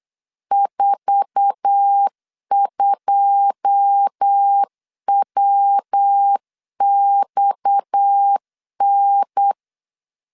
Morse code, heard as a keyed tone, about 8 wpm, 800 Hz. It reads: 42WXN
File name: LEZIONI / 10 SIGNS / 10A SLASH 8wpm 3452